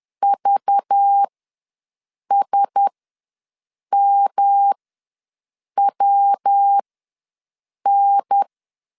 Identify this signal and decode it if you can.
Morse code, heard as a keyed tone, about 10 wpm, spaced as for 6 wpm, 800 Hz. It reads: VSMWN
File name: LEZIONI / 5 LBVC / 5 LBV ws 10wpm 3645